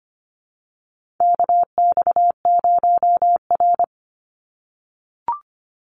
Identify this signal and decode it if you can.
Morse code, heard as a keyed tone, about 25 wpm, 700 Hz. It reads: KX0R
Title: NA SOTA Activator Callsigns 25wpm